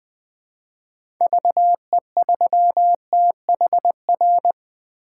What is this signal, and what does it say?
Morse code, heard as a keyed tone, about 21 wpm, 700 Hz. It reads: VE3THR
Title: 2023 POTA Hunter Call Signs - 20wpm